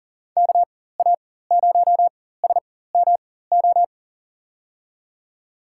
Morse code, heard as a keyed tone, about 40 wpm, spaced as for 19 wpm, 700 Hz. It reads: KA0SMO
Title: USA Call Signs - 4x Character Spacing - 40wpm